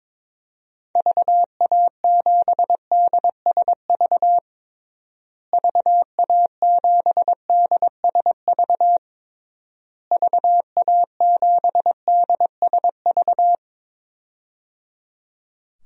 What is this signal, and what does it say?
Morse code, heard as a keyed tone, about 22 wpm, 700 Hz. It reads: VA7DSV VA7DSV VA7DSV
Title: DXCC-100 Call Signs - 6 Characters - Repeated 3x - 22wpm